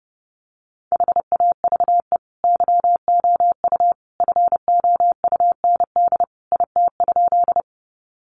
Morse code, heard as a keyed tone, about 30 wpm, 700 Hz. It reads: HAVE YOU FOUND IT?
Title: Sentences from Top 300 Words 30wpm